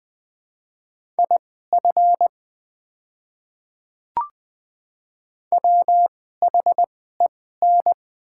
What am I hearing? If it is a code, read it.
Morse code, heard as a keyed tone, about 20 wpm, spaced as for 14 wpm, 700 Hz. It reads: IF WHEN
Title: 2x Character Spacing - Top 100 Words - 20wpm